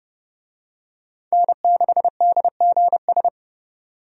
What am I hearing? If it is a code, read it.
Morse code, heard as a keyed tone, about 30 wpm, 700 Hz. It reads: N6DGS
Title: USA Call Signs - 5 Characters - 30wpm